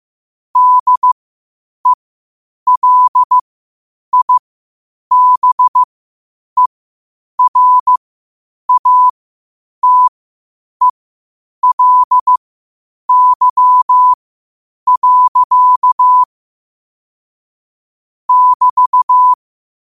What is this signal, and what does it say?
Morse code, heard as a keyed tone, about 15 wpm, spaced as for 8 wpm, 1000 Hz. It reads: DELIBERATELY. <BT>
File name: qotdfiles / 8wpm